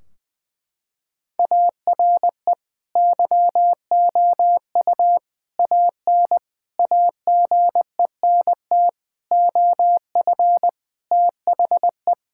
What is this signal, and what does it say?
Morse code, heard as a keyed tone, about 20 wpm, 700 Hz. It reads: ARE YOU AN AGENT OF THE
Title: Sentences from Top 1600 Words 20wpm